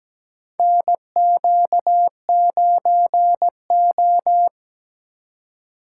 Morse code, heard as a keyed tone, about 17 wpm, 700 Hz. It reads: NQ9O
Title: USA Call Signs - 4 Characters - 4x Character Spacing - 17wpm